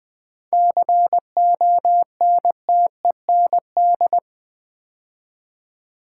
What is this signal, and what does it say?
Morse code, heard as a keyed tone, about 20 wpm, 700 Hz. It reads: CONTEND
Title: 7 Letter Words - First 3 Letters Given - 20wpm